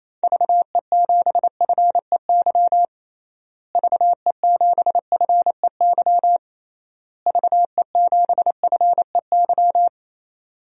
Morse code, heard as a keyed tone, about 27 wpm, 700 Hz. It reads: VE7FEY VE7FEY VE7FEY
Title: DXCC-100 Call Signs - 6 Characters - Repeated 3x - 28wpm